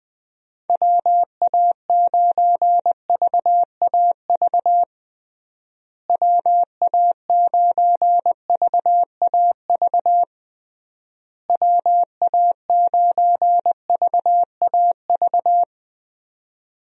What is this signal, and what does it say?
Morse code, heard as a keyed tone, about 20 wpm, 700 Hz. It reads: WA9VAV WA9VAV WA9VAV
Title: USA Call Signs - 6 Characters - Repeated 3x - 20wpm